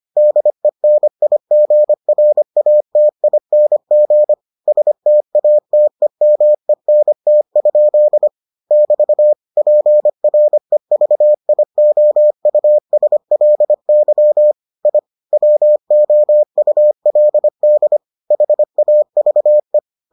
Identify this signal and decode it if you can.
Morse code, heard as a keyed tone, about 25 wpm, 600 Hz. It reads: DENIGRATING STATEMENT? <BT> PREVIOUSLY I WOULD HAVE